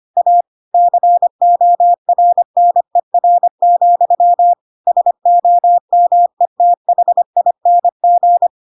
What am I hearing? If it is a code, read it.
Morse code, heard as a keyed tone, about 25 wpm, 700 Hz. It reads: A CORNER, SOMETHING
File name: Pinocchio / Pinocchio ch5 25 25 ik1hge